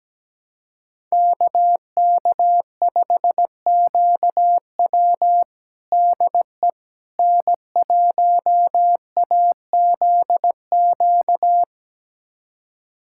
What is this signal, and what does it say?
Morse code, heard as a keyed tone, about 17 wpm, 700 Hz. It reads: KK5QW DE N1AZQ
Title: USA Callsign DE Callsign - 5 Characters - 17wpm